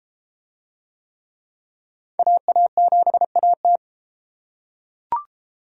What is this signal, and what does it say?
Morse code, heard as a keyed tone, about 28 wpm, 700 Hz. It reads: AA7AT